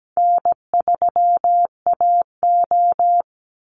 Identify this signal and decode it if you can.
Morse code, heard as a keyed tone, about 17 wpm, 700 Hz. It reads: N3AO
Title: Call Signs - 17wpm